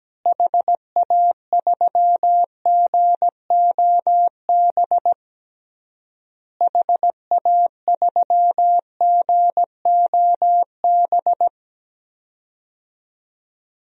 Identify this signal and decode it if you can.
Morse code, heard as a keyed tone, about 17 wpm, 700 Hz. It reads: HA3GOB HA3GOB
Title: DXCC-100 Call Signs - 6 Characters - Repeated 3x - 17wpm